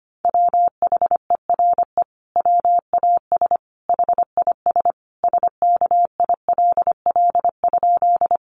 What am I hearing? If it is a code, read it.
Morse code, heard as a keyed tone, about 25 wpm, 700 Hz. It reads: WHERE WAS HIS SKILL?
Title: Sentences from Top 1200 Words 25wpm